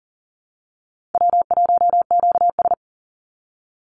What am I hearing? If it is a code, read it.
Morse code, heard as a keyed tone, about 40 wpm, 700 Hz. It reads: W1QS
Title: Easy Call Signs - 40wpm